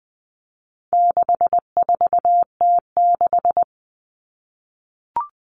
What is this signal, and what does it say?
Morse code, heard as a keyed tone, about 20 wpm, 700 Hz. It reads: 64T6
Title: Common Cut Numbers - 4-Digit 20wpm